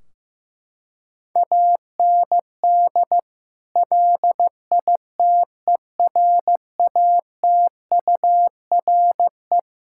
Morse code, heard as a keyed tone, about 15 wpm, 700 Hz. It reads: AND LITERATURE
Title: Sentences from Top 1600 Words 15wpm